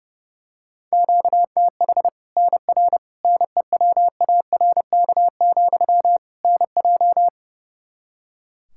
Morse code, heard as a keyed tone, about 32 wpm, 700 Hz. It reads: QTH NR NEWARK, NJ
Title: Practice QSO - Exchange 1 - Beginning - 30wpm